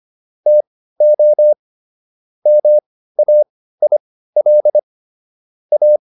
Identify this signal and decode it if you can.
Morse code, heard as a keyed tone, about 25 wpm, spaced as for 15 wpm, 600 Hz. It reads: TO MAIL A